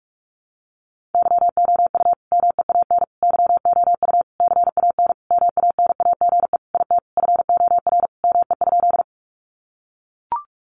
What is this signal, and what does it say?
Morse code, heard as a keyed tone, about 44 wpm, 700 Hz. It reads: YOU MEAN YOU CAN MANAGE IT FOR ME?
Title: Sentences from Top 1400 Words 45wpm